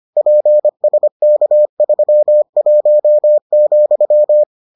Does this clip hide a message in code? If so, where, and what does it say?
Yes: Morse code from 0.2 to 4.4 s, heard as a keyed tone, about 25 wpm, 600 Hz. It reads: PSK31,